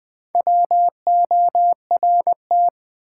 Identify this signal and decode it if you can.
Morse code, heard as a keyed tone, about 20 wpm, 700 Hz. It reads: WORT
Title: Build A Word - 4 Letters At A Time - 10-Letter Words - 20wpm